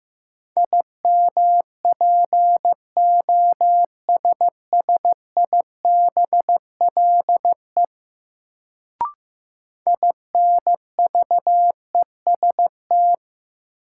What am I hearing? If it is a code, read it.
Morse code, heard as a keyed tone, about 15 wpm, 700 Hz. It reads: IMPOSSIBLE INVEST
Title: Sentences from Top 1100 Words - New Word Drill Alt - 15wpm